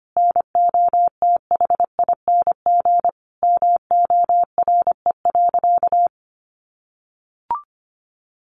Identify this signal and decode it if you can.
Morse code, heard as a keyed tone, about 25 wpm, 700 Hz. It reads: NOTHING MORE.
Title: Sentences from Top 1400 Words 25wpm